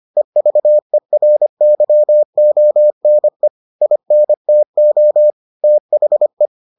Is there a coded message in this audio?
Yes: Morse code from 0.2 to 6.5 s, heard as a keyed tone, about 25 wpm, 600 Hz. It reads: EVERYONE INTO THE